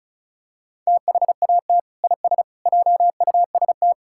Morse code, heard as a keyed tone, about 36 wpm, 700 Hz. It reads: THAT IS JUST
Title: Sentences from Top 1600 Words 35wpm